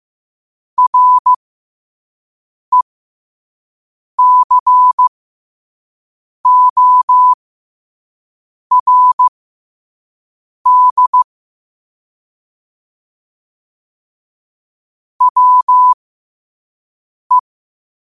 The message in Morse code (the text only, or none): RECORD WE